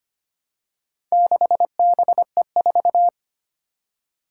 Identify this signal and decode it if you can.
Morse code, heard as a keyed tone, about 25 wpm, 700 Hz. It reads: 6BE4
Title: Four Challenging Letter-Number Combinations 25wpm